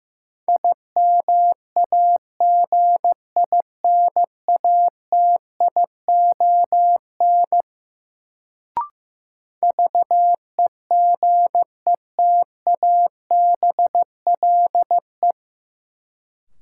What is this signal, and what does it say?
Morse code, heard as a keyed tone, about 15 wpm, 700 Hz. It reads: IMAGINATION VEGETABLE
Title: Sentences from Top 1300 Words - New Word Drill - 15wpm